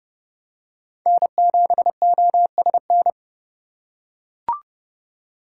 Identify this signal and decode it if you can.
Morse code, heard as a keyed tone, about 29 wpm, 700 Hz. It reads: N7OSN
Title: USA Call Signs - 5 Characters - 2x Character Spacing - 30wpm